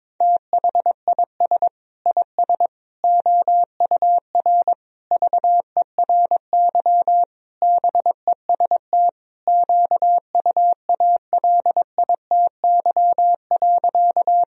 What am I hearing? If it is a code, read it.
Morse code, heard as a keyed tone, about 21 wpm, 700 Hz. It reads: THIS IS OUR VERY BEST QUALITY.
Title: Sentences from Top 1000 Words - Review Part 4 - 22wpm